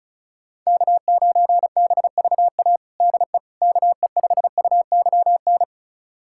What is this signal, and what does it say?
Morse code, heard as a keyed tone, about 36 wpm, 700 Hz. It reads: K9BVA DE KE5UYN
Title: USA Callsign DE Callsign 35wpm